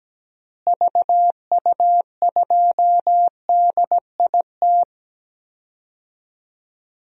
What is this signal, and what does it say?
Morse code, heard as a keyed tone, about 17 wpm, 700 Hz. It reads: VU2DIT
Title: DXCC-100 Call Signs - 6 Characters - Repeated 3x - 17wpm